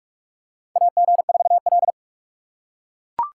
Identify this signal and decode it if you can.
Morse code, heard as a keyed tone, about 44 wpm, 700 Hz. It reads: AG4L
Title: USA Call Signs - 4 Characters - 6x Character Spacing - 45wpm